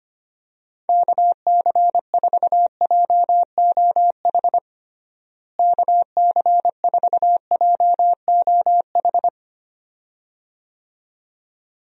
Morse code, heard as a keyed tone, about 25 wpm, 700 Hz. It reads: KC4JOH KC4JOH
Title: USA Call Signs - 6 Characters - Repeated 2x - 25wpm